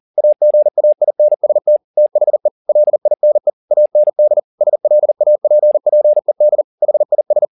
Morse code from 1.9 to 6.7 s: THE LINE AND SLAPPED

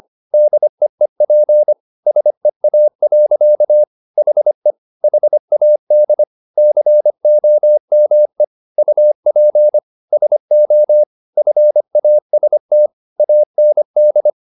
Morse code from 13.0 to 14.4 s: AND

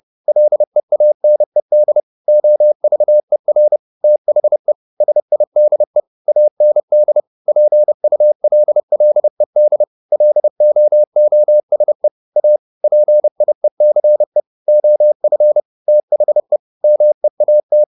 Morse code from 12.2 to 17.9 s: A PIECE OF THE MEAT

